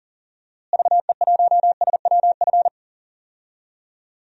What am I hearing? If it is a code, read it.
Morse code, heard as a keyed tone, about 40 wpm, 700 Hz. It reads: VE1SWF
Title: DXCC-100 Call Signs - 6 Characters - Repeated 2x - 40wpm